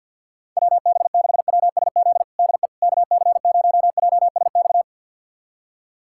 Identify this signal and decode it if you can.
Morse code, heard as a keyed tone, about 52 wpm, 700 Hz. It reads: WB6WSZ DE KC0JSX
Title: USA Callsign DE Callsign 50wpm